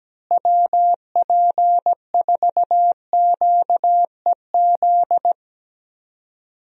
Morse code, heard as a keyed tone, about 17 wpm, 700 Hz. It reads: WP4QEZ
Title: USA Call Signs - 6 Characters - Repeated 2x - 17wpm